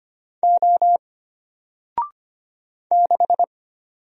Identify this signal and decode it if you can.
Morse code, heard as a keyed tone, about 25 wpm, 700 Hz. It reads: O 6
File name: Single-Letter-Number-25wpm